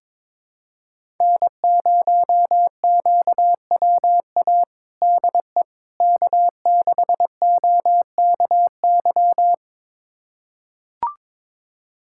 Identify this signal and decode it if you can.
Morse code, heard as a keyed tone, about 22 wpm, 700 Hz. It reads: N0QWA DE K6OKY
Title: USA Callsign DE Callsign - 5 Characters - 22wpm